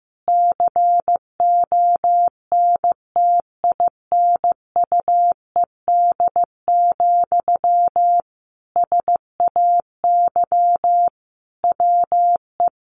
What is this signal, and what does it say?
Morse code, heard as a keyed tone, about 15 wpm, 700 Hz. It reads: CONTINUED, SAY WE
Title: A Sound Of Thunder 15wpm